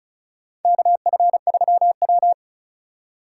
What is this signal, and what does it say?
Morse code, heard as a keyed tone, about 36 wpm, 700 Hz. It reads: KF3W
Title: USA Call Signs - 4 Characters - 2x Character Spacing - 35wpm